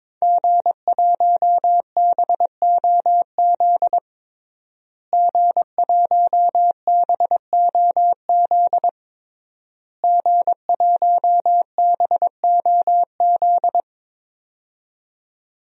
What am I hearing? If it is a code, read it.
Morse code, heard as a keyed tone, about 22 wpm, 700 Hz. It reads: G1BOZ G1BOZ G1BOZ
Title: DXCC-100 Call Signs - 5 Characters - Repeated 3x - 22wpm